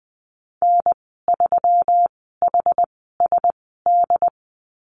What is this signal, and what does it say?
Morse code, heard as a keyed tone, about 20 wpm, spaced as for 14 wpm, 700 Hz. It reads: N3HSD
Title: 167 Lesson 27g - Number 7 - Call Signs 20wpm